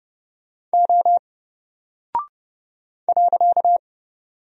Morse code, heard as a keyed tone, about 30 wpm, 700 Hz. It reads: O .
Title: Single Character (Letter:Number:Punctuation:Prosign) Alt 30wpm